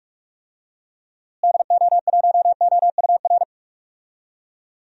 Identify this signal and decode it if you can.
Morse code, heard as a keyed tone, about 46 wpm, 700 Hz. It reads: DO1OUR
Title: DXCC-100 Call Signs - 6 Characters - Repeated 2x - 45wpm